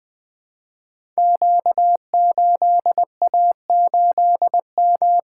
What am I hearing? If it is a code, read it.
Morse code, heard as a keyed tone, about 20 wpm, 700 Hz. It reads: Q8A8M
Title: Code Groups - 5 Letter:Numbers - 20wpm